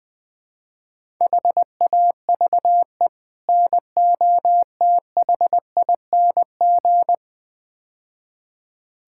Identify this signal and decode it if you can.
Morse code, heard as a keyed tone, about 21 wpm, 700 Hz. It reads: HAVE NOTHING
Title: Build A Phrase - 5-Word Phrases - 20wpm